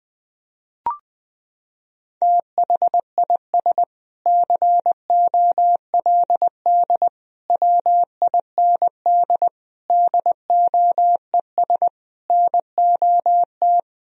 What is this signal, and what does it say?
Morse code, heard as a keyed tone, about 21 wpm, 700 Hz. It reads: THIS COLD WIND DOES NOT
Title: Sentences from Top 1000 Words - Review Part 2 - 20wpm